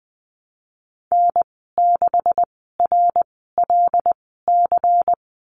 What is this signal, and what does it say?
Morse code, heard as a keyed tone, about 20 wpm, spaced as for 15 wpm, 700 Hz. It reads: N6RLC
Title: 140 Lesson 23g - Letter f - Call Signs 20wpm